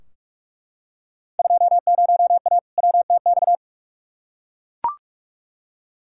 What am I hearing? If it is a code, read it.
Morse code, heard as a keyed tone, about 41 wpm, 700 Hz. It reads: 20A WTX
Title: Field Day Exchanges Encore - 45wpm